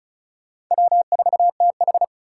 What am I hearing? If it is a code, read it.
Morse code, heard as a keyed tone, about 44 wpm, 700 Hz. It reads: W4TH